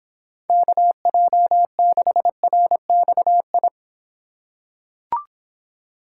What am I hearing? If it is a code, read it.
Morse code, heard as a keyed tone, about 26 wpm, 700 Hz. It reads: KJ6RXI